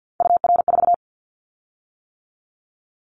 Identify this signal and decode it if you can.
Morse code, heard as a keyed tone, about 50 wpm, 700 Hz. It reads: UR4
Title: Callsign Prefixes for Top 25 DXCC - 50wpm